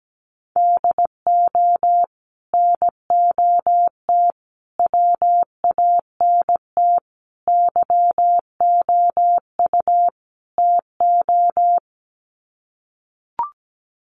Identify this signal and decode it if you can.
Morse code, heard as a keyed tone, about 17 wpm, 700 Hz. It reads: DO NOT WANT YOU TO
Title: Sets of 5 Words 17wpm